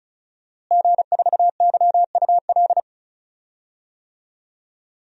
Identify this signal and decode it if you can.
Morse code, heard as a keyed tone, about 36 wpm, 700 Hz. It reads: G4YUL
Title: DXCC-100 Call Signs - 5 Characters - Repeated 3x - 35wpm